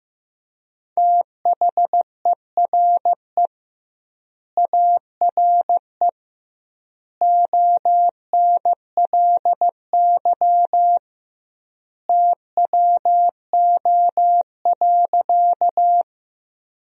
Morse code, heard as a keyed tone, about 15 wpm, 700 Hz. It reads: THERE ARE ONLY TWO.
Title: 2x Word-Spacing - Sentences from Top 200 Words - 15wpm